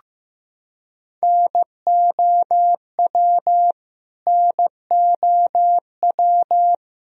Morse code, heard as a keyed tone, about 16 wpm, 700 Hz. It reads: NOW NOW
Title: Top 100 Words Repeated 15wpm